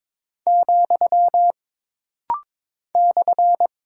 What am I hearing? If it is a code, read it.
Morse code, heard as a keyed tone, about 22 wpm, 700 Hz. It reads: , /
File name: practice / 22wpm / Single Punctuation-Prosign - Rapid-Fire - 22wpm